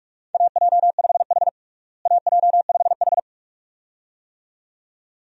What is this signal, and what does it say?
Morse code, heard as a keyed tone, about 46 wpm, 700 Hz. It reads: AJ5H AJ5H
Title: USA Call Signs - 4 Characters - Repeated 2x - 45wpm